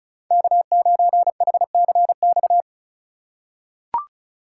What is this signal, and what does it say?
Morse code, heard as a keyed tone, about 36 wpm, 700 Hz. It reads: K9HCX
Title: USA Call Signs - 5 Characters - 3x Character Spacing - 35wpm